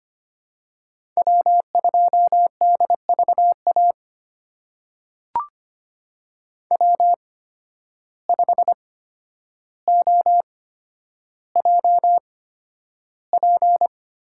Morse code, heard as a keyed tone, about 25 wpm, spaced as for 11 wpm, 700 Hz. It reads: W2DVA W5OJP
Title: USA Call Signs - 5 Characters - 8x Character Spacing - 25wpm